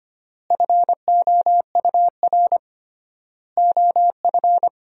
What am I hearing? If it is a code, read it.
Morse code, heard as a keyed tone, about 25 wpm, 700 Hz. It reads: FOUR OF
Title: 3x Word-Spacing - Sentences from Top 300 Words - 25wpm